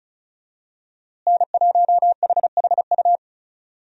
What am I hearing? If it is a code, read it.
Morse code, heard as a keyed tone, about 36 wpm, 700 Hz. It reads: N1HHU
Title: USA Call Signs - 5 Characters - Repeated 3x - 35wpm